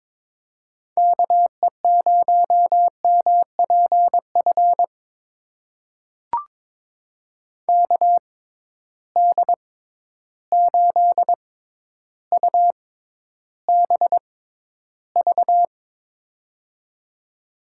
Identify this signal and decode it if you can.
Morse code, heard as a keyed tone, about 22 wpm, spaced as for 11 wpm, 700 Hz. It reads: KE0MPF KD8UBV
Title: USA Call Signs - 6 Characters - 6x Character Spacing - 22wpm